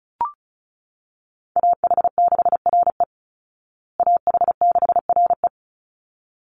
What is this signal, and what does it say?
Morse code, heard as a keyed tone, about 34 wpm, 700 Hz. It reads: AH6RE AH6RE
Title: Call Signs Repeated 3x - 35wpm